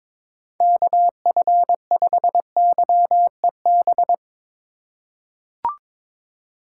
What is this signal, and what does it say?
Morse code, heard as a keyed tone, about 22 wpm, 700 Hz. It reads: KF5YEB
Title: USA Call Signs - 6 Characters - 3x Character Spacing - 22wpm